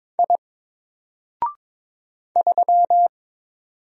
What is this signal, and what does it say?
Morse code, heard as a keyed tone, about 22 wpm, 700 Hz. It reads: I 3
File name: Single-Letter-Number-22wpm